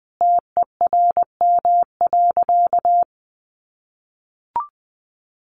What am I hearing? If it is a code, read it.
Morse code, heard as a keyed tone, about 20 wpm, 700 Hz. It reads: TERM.
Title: Sentences from Top 1100 Words 20wpm